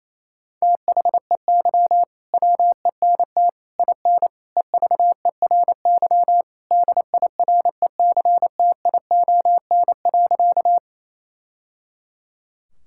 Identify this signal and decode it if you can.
Morse code, heard as a keyed tone, about 28 wpm, 700 Hz. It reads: THEY WENT IN EVERY DIRECTION.
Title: Sentences from Top 1000 Words - Review Part 5 - 28wpm